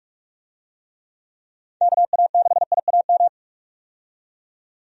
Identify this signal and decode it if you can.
Morse code, heard as a keyed tone, about 46 wpm, 700 Hz. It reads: KA6IAM
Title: USA Call Signs - 6 Characters - 45wpm